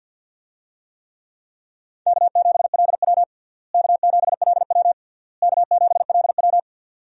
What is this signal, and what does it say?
Morse code, heard as a keyed tone, about 48 wpm, 700 Hz. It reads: K7LW K7LW K7LW
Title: USA Call Signs - 4 Characters - Repeated 3x - 50wpm